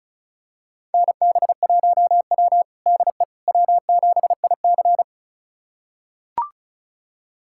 Morse code, heard as a keyed tone, about 34 wpm, 700 Hz. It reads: NB1W DE W7IC
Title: USA Callsign DE Callsign - 4 Characters - 35wpm